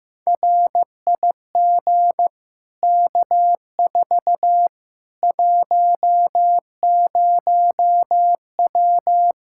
Morse code, heard as a keyed tone, about 16 wpm, 700 Hz. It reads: RIG K4 10W
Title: K1USN SST QSO Practice 15wpm